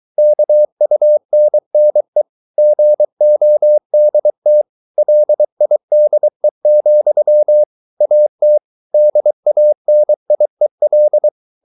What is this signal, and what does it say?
Morse code, heard as a keyed tone, about 23 wpm, 600 Hz. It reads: KUNNE GODT LIDE, AT DANIEL
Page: Ferie 23 wpm